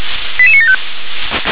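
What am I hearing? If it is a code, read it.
Five-tone selective call (ZVEI), heard as quick runs of five tones, tones of about 70 ms. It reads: 88975
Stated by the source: ZVEI five-tone call received on an SDR.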